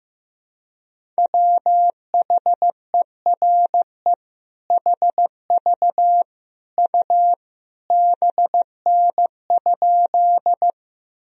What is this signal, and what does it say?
Morse code, heard as a keyed tone, about 16 wpm, 700 Hz. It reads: WHERE HV U BN?
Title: Sentences from Top 200 Words and Abbreviations 15wpm